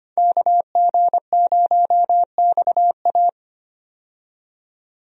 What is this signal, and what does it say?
Morse code, heard as a keyed tone, about 25 wpm, 700 Hz. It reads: KG0XA
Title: USA Call Signs - 5 Characters - 25wpm